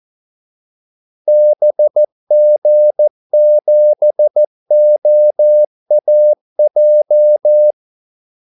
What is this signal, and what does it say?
Morse code, heard as a keyed tone, about 14 wpm, 600 Hz. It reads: BG7OAJ
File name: Callsigns-14wpm